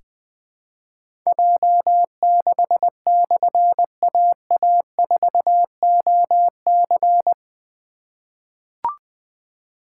Morse code, heard as a keyed tone, about 20 wpm, 700 Hz. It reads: J6/AA4OC
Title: Difficult Call Signs 20wpm